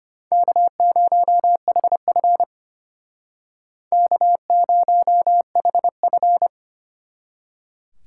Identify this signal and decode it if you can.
Morse code, heard as a keyed tone, about 32 wpm, 700 Hz. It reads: K0HF K0HF